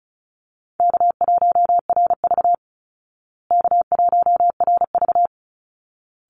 Morse code, heard as a keyed tone, about 34 wpm, 700 Hz. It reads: K1RV K1RV
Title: Easy Call Signs Repeated 2x - 35wpm